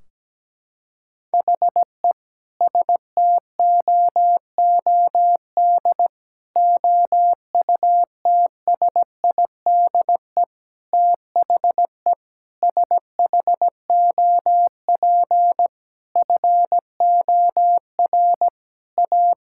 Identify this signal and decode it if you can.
Morse code, heard as a keyed tone, about 17 wpm, 700 Hz. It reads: HE STOOD OUTSIDE THE SHOP FOR A
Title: Sentences from Top 1600 Words 17wpm